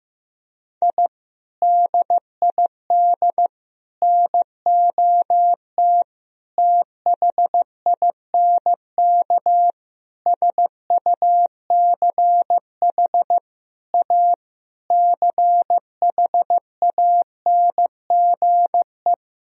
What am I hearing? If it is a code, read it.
Morse code, heard as a keyed tone, about 15 wpm, 700 Hz. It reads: I DID NOT THINK SUCH A CHANGE
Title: Sentences from Top 1000 Words - Review Part 1 - 15wpm